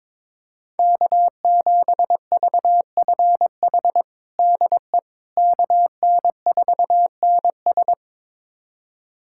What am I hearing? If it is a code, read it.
Morse code, heard as a keyed tone, about 22 wpm, 700 Hz. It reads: K7VFH DE KN4NS
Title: USA Callsign DE Callsign - 5 Characters - 22wpm